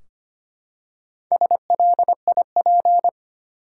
Morse code, heard as a keyed tone, about 25 wpm, 700 Hz. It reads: SLIP
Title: Four Letter Words - 25wpm